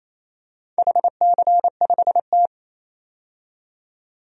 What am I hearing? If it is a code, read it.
Morse code, heard as a keyed tone, about 28 wpm, 700 Hz. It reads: HC5T
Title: DXCC-100 Call Signs - 4 Characters - 28wpm